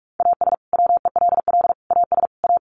Morse code, heard as a keyed tone, about 44 wpm, 700 Hz. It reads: AS WELL AS A
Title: Sets of 4 Words 45wpm